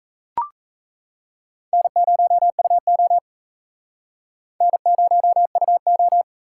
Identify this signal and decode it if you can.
Morse code, heard as a keyed tone, about 37 wpm, 700 Hz. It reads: N0UO N0UO